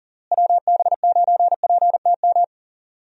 Morse code, heard as a keyed tone, about 40 wpm, 700 Hz. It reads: WB9PTM
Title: USA Call Signs - 6 Characters - Repeated 3x - 40wpm